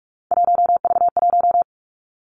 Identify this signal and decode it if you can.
Morse code, heard as a keyed tone, about 46 wpm, 700 Hz. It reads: 1V1
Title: Three Challenging Letter-Number Combinations 45wpm